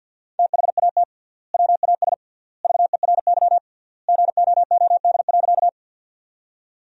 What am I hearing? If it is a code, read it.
Morse code, heard as a keyed tone, about 53 wpm, 700 Hz. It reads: THAT WAS VERY GOOD.
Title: 3x Word-Spacing - Sentences from Top 200 Words - 50wpm